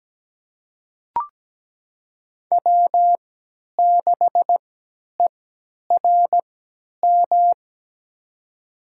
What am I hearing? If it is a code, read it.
Morse code, heard as a keyed tone, about 17 wpm, spaced as for 10 wpm, 700 Hz. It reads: W6ERM
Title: USA Call Signs - 5 Characters - 3x Character Spacing - 17wpm